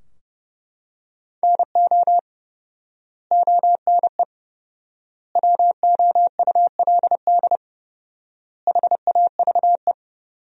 Morse code, heard as a keyed tone, about 31 wpm, 700 Hz. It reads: NO ONE WOULD HAVE
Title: 4x Word-Spacing - Sentences from Top 300 Words - 30wpm